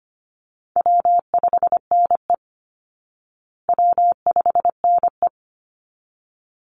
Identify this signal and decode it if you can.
Morse code, heard as a keyed tone, about 25 wpm, 700 Hz. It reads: W5NE W5NE
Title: Easy Call Signs Repeated 2x - 25wpm